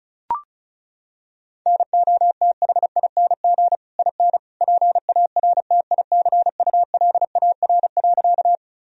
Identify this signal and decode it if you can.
Morse code, heard as a keyed tone, about 34 wpm, 700 Hz. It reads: NOTHING IN PARTICULAR.
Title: Sentences from Top 1000 Words - Review Part 4 - 35wpm